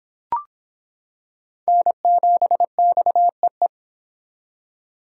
Morse code, heard as a keyed tone, about 26 wpm, 700 Hz. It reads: N7XEE